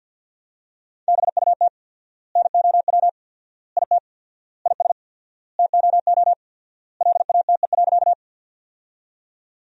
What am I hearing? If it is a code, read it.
Morse code, heard as a keyed tone, about 50 wpm, 700 Hz. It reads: BUT NOW IT IS TOO LATE.
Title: 4x Word-Spacing - Sentences from Top 300 Words - 50wpm